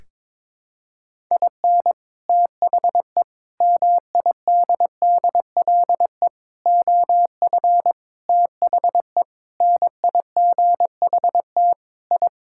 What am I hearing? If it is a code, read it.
Morse code, heard as a keyed tone, about 23 wpm, 700 Hz. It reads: IN THE MIDDLE OF THE NIGHT I
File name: practice / 22wpm / Sentences with 4-Word Phrases - 22wpm